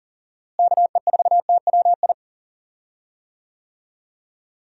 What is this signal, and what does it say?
Morse code, heard as a keyed tone, about 40 wpm, 700 Hz. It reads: KE4TWI
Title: USA Call Signs - 6 Characters - Repeated 3x - 40wpm